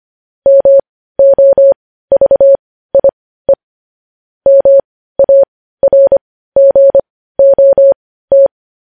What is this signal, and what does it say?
Morse code, heard as a keyed tone, about 25 wpm, spaced as for 15 wpm, 550 Hz. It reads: MOVIE MARGOT